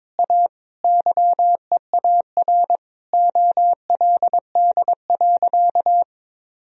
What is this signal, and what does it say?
Morse code, heard as a keyed tone, about 22 wpm, 700 Hz. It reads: A YEAR OLD.
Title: Sentences from Top 1000 Words - Review Part 1 - 22wpm